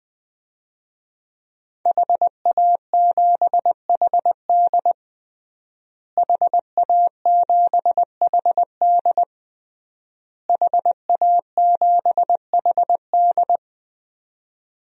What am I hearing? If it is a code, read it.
Morse code, heard as a keyed tone, about 21 wpm, 700 Hz. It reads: HA7HD HA7HD HA7HD
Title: DXCC-100 Call Signs - 5 Characters - Repeated 3x - 20wpm